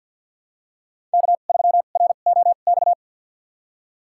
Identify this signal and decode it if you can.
Morse code, heard as a keyed tone, about 45 wpm, spaced as for 34 wpm, 700 Hz. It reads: K3ROX
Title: USA Call Signs - 5 Characters - 2x Character Spacing - 50wpm